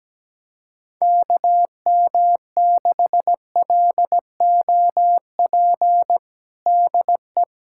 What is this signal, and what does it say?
Morse code, heard as a keyed tone, about 17 wpm, 700 Hz. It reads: KM6LOP DE
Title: USA Callsign DE Callsign - 6 Characters - 17wpm